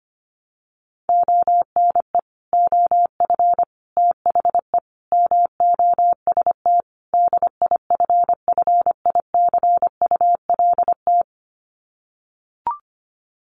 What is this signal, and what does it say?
Morse code, heard as a keyed tone, about 25 wpm, 700 Hz. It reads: ONE OF THE MOST DIFFICULT
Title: Sets of 5 Words Encore2 25wpm